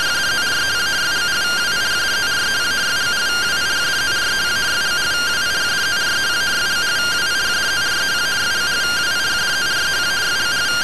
Two-tone teleprinter signal, about 170 Hz shift.